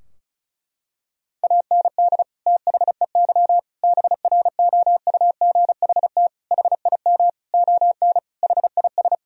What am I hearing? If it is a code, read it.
Morse code, heard as a keyed tone, about 36 wpm, 700 Hz. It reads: AND THEY BROUGHT HIM ON HIS
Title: Sentences from Top 1000 Words - Review Part 2 - 35wpm